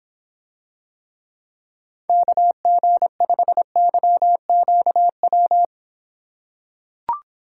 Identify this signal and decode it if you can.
Morse code, heard as a keyed tone, about 26 wpm, 700 Hz. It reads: KG5YQW